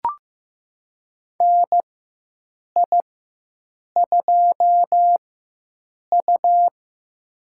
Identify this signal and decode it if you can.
Morse code, heard as a keyed tone, about 15 wpm, spaced as for 7 wpm, 700 Hz. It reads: NI2U
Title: USA Call Signs - 4 Characters - 4x Character Spacing - 15wpm